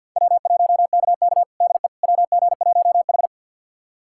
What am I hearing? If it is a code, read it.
Morse code, heard as a keyed tone, about 48 wpm, 700 Hz. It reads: W1KK DE WG1H
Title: USA Callsign DE Callsign - 4 Characters - 50wpm